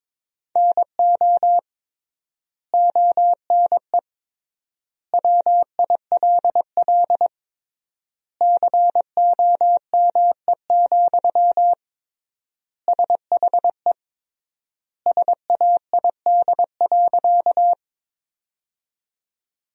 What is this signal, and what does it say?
Morse code, heard as a keyed tone, about 21 wpm, 700 Hz. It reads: NO ONE WILL COME, SHE SAID.
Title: 3x Word-Spacing - Sentences from Top 100 Words - 22wpm